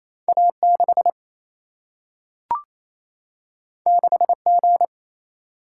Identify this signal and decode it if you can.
Morse code, heard as a keyed tone, about 28 wpm, 700 Hz. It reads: A 6 6 G
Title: Single-Letter Single-Number 28wpm